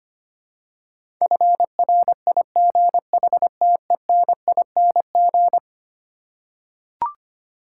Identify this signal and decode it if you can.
Morse code, heard as a keyed tone, about 25 wpm, 700 Hz. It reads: FRIGHTENING
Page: morse-code-ninja-practice-25wpm